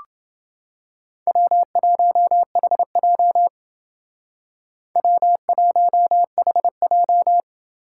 Morse code, heard as a keyed tone, about 28 wpm, 700 Hz. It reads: W1HJ W1HJ